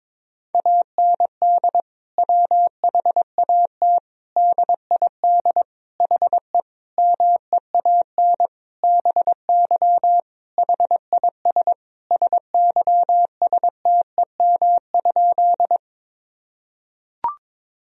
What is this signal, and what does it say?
Morse code, heard as a keyed tone, about 22 wpm, 700 Hz. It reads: AND WHAT DID HE MEAN BY HIS SYSTEM?
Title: Sentences from Top 1000 Words - Review Part 3 - 22wpm